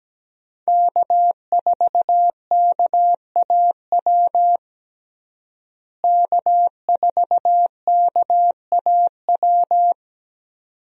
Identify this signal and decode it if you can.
Morse code, heard as a keyed tone, about 17 wpm, 700 Hz. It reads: K4KAW K4KAW
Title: USA Call Signs - 5 Characters - Repeated 2x - 17wpm